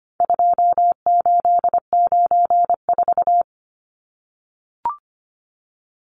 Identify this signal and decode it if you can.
Morse code, heard as a keyed tone, about 25 wpm, 700 Hz. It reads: 2894
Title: Numbers - 4-Digit 25wpm